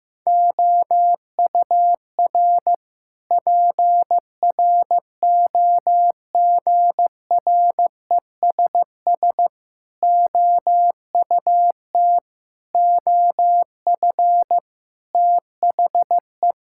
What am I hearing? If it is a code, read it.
Morse code, heard as a keyed tone, about 15 wpm, 700 Hz. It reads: OUR PROGRESS OUT OF THE
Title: Sentences from Top 1600 Words 15wpm